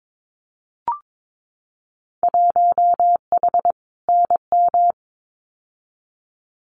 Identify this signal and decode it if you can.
Morse code, heard as a keyed tone, about 22 wpm, 700 Hz. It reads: 1H NM
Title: Winter Field Day Exchanges 22wpm